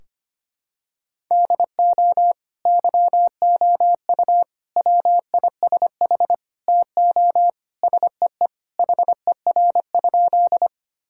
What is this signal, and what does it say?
Morse code, heard as a keyed tone, about 25 wpm, 700 Hz. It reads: DO YOU WISH TO SEE HER?
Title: Sentences from Top 1600 Words 25wpm